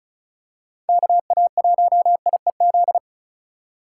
Morse code, heard as a keyed tone, about 34 wpm, 700 Hz. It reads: KA1IEZ
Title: USA Call Signs - 6 Characters - 4x Character Spacing - 35wpm